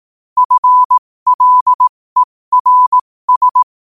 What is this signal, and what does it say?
Morse code, heard as a keyed tone, about 18 wpm, spaced as for 15 wpm, 1000 Hz. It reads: FLERS
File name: qotdfiles / 15wpm